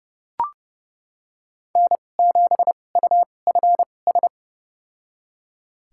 Morse code, heard as a keyed tone, about 30 wpm, spaced as for 22 wpm, 690 Hz. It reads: N7UFS
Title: USA Call Signs - 5 Characters - 2x Character Spacing - 30wpm